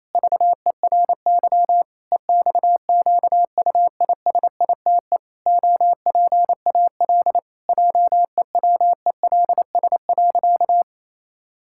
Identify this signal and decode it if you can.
Morse code, heard as a keyed tone, about 28 wpm, 700 Hz. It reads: VERY EXQUISITE OPAL JEWELS.
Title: Pangram Challenge v2 - 28wpm